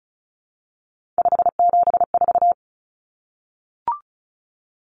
Morse code, heard as a keyed tone, about 34 wpm, 700 Hz. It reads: 574
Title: Common Cut Numbers - 3-Digit 35wpm